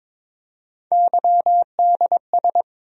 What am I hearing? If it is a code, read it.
Morse code, heard as a keyed tone, about 21 wpm, 700 Hz. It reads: YDS
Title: USA Call Sign Postfixes - 3 Letters 22wpm